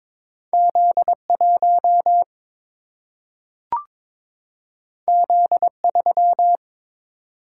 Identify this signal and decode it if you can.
Morse code, heard as a keyed tone, about 22 wpm, 700 Hz. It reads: Z 1 Z 3
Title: Single-Letter Single-Number 22wpm